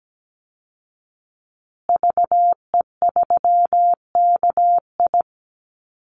Morse code, heard as a keyed tone, about 17 wpm, 700 Hz. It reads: VE3KI
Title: Call Signs Repeated 2x - 17wpm